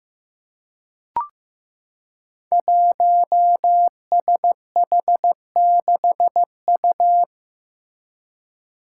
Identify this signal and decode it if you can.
Morse code, heard as a keyed tone, about 15 wpm, 700 Hz. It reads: 1SH6U
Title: Five Challenging Letter-Number Combinations 15wpm